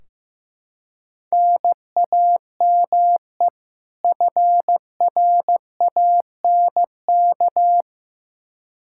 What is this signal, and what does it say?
Morse code, heard as a keyed tone, about 15 wpm, 700 Hz. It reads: NAME FRANK
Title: Practice QSO - Exchange 1 - Beginning - 15wpm